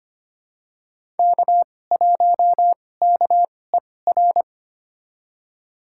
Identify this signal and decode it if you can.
Morse code, heard as a keyed tone, about 25 wpm, spaced as for 18 wpm, 700 Hz. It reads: K1KER
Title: USA Call Signs - 5 Characters - 2x Character Spacing - 25wpm